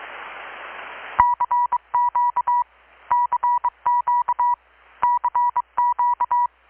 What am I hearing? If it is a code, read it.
Morse code, heard as a keyed tone, about 24 wpm, 970 Hz. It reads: CQ CQ CQ
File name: blog / swl / 6-abril-2019-12695-USB-KPS